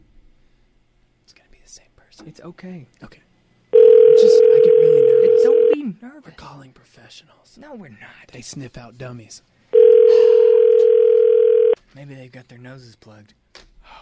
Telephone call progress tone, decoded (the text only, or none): ringback tone